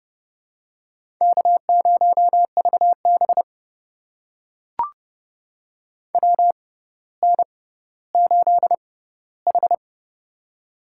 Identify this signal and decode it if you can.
Morse code, heard as a keyed tone, about 31 wpm, spaced as for 15 wpm, 700 Hz. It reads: K0VB WN8H
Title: USA Call Signs - 4 Characters - 6x Character Spacing - 30wpm